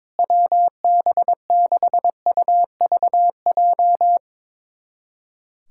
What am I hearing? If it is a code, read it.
Morse code, heard as a keyed tone, about 22 wpm, 700 Hz. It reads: WB6UVJ
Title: USA Callsign DE Callsign - 6 Characters - 22wpm